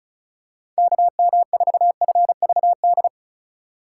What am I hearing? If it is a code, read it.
Morse code, heard as a keyed tone, about 36 wpm, 700 Hz. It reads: KM4FVD
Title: USA Call Signs - 6x Character Spacing - 35wpm